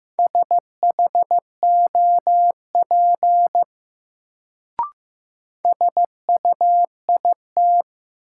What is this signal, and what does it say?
Morse code, heard as a keyed tone, about 15 wpm, 700 Hz. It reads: SHOP SUIT
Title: Sentences from Top 1600 Words - New Word Drill Alt - 15wpm